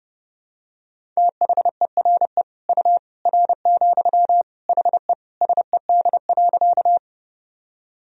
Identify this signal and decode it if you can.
Morse code, heard as a keyed tone, about 31 wpm, 700 Hz. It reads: THERE U R, HE SED.
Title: Sentences from Top 100 Words and Abbreviations 30wpm